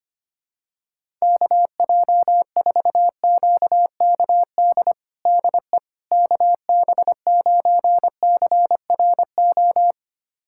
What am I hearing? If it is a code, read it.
Morse code, heard as a keyed tone, about 25 wpm, 700 Hz. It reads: KJ4QKD DE KB9CRO
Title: USA Callsign DE Callsign - 6 Characters - 25wpm